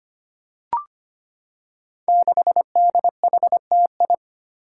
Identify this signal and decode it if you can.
Morse code, heard as a keyed tone, about 25 wpm, 700 Hz. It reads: 6DHTI
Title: Five Challenging Letter-Number Combinations 25wpm